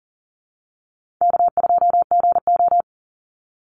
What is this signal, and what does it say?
Morse code, heard as a keyed tone, about 40 wpm, 700 Hz. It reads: K2GO
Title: Call Signs Repeated 3x - 40wpm